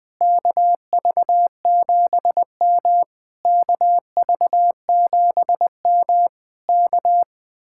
Morse code, heard as a keyed tone, about 21 wpm, 700 Hz. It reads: KV7M KV7M K
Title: Speed-Racing Common QSO Phrases Encore - 20wpm